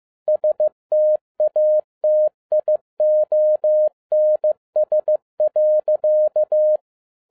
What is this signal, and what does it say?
Morse code, heard as a keyed tone, about 15 wpm, 610 Hz. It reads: STATIONS.
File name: 2011-07-mp3 / n22 QRP OTAKU 15wpm